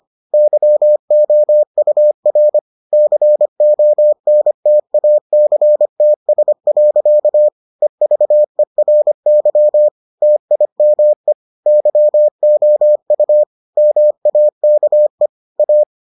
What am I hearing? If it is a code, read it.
Morse code, heard as a keyed tone, about 25 wpm, 600 Hz. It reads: YOUR CONTACTS. EVERY TIME YOU MAKE A